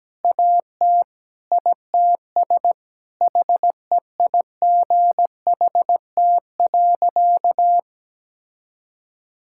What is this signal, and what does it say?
Morse code, heard as a keyed tone, about 17 wpm, 700 Hz. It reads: AT ITS HEIGHT.
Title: Sentences from Top 1600 Words 17wpm